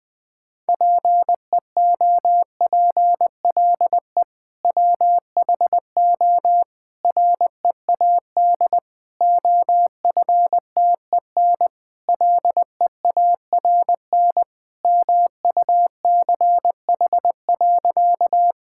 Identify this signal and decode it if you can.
Morse code, heard as a keyed tone, about 21 wpm, 700 Hz. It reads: PEOPLE WHO READ OFTEN LEARN MUCH.
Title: Sentences with 2-Word Phrases - 20wpm